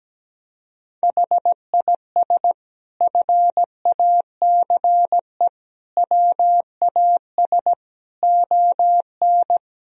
Morse code, heard as a keyed tone, about 17 wpm, 700 Hz. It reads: HIS FACE WAS ON
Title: Sentences from Top 1000 Words - Review Part 2 - 17wpm